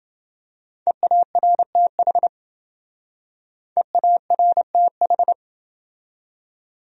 Morse code, heard as a keyed tone, about 27 wpm, 700 Hz. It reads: EARTH EARTH